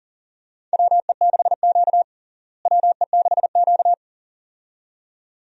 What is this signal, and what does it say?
Morse code, heard as a keyed tone, about 40 wpm, 700 Hz. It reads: WE6Q WE6Q
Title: USA Call Signs - 4 Characters - Repeated 2x - 40wpm